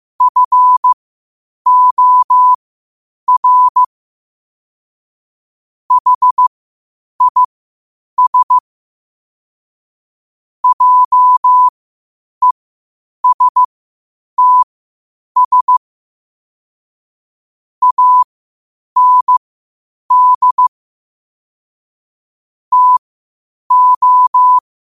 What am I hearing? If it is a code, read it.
Morse code, heard as a keyed tone, about 15 wpm, spaced as for 8 wpm, 1000 Hz. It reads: FOR HIS JESTS AND TO
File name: qotdfiles / 8wpm